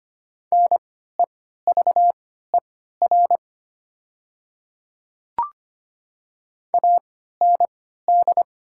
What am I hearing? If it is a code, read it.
Morse code, heard as a keyed tone, about 25 wpm, spaced as for 14 wpm, 690 Hz. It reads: NEVER AND
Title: 3x Character Spacing - Top 500 Words - 25wpm